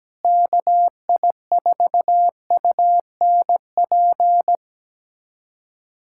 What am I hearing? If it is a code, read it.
Morse code, heard as a keyed tone, about 17 wpm, 700 Hz. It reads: KI4UNP
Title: USA Callsign DE Callsign - 6 Characters - 17wpm